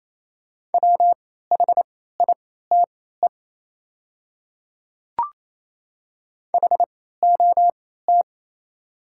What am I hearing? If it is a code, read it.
Morse code, heard as a keyed tone, about 28 wpm, spaced as for 16 wpm, 700 Hz. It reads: WHITE HOT
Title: 3x Character Spacing - Top 300 Words - 28wpm